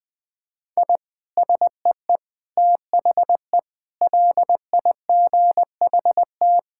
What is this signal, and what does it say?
Morse code, heard as a keyed tone, about 21 wpm, 700 Hz. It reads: I SEE THE LIGHT
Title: Sentences with 8-Word Phrases - 20wpm